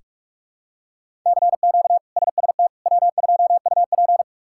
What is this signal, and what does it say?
Morse code, heard as a keyed tone, about 42 wpm, 700 Hz. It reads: CQ SST W2UP
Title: K1USN SST QSO Practice - 45wpm